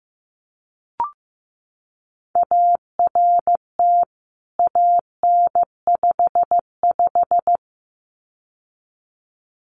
Morse code, heard as a keyed tone, about 15 wpm, 700 Hz. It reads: ART AN55
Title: CWT Exchange 15wpm